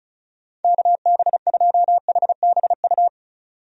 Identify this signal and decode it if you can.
Morse code, heard as a keyed tone, about 36 wpm, 700 Hz. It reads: KB2HBU
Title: USA Call Signs - 6 Characters - 35wpm